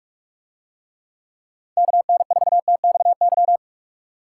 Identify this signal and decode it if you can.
Morse code, heard as a keyed tone, about 43 wpm, 700 Hz. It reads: KN4TXY